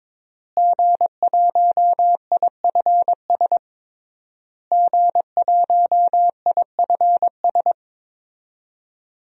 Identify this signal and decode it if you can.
Morse code, heard as a keyed tone, about 22 wpm, 700 Hz. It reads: G1IFS G1IFS
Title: DXCC-100 Call Signs - 5 Characters - Repeated 3x - 22wpm